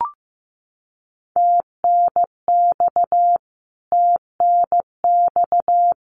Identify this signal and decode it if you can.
Morse code, heard as a keyed tone, about 15 wpm, 700 Hz. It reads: TNX TNX
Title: Abbreviations Repeated 3x - 15wpm